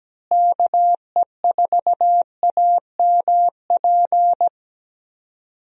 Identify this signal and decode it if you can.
Morse code, heard as a keyed tone, about 17 wpm, 700 Hz. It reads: KE4AMP
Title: USA Call Signs - 6 Characters - Repeated 2x - 17wpm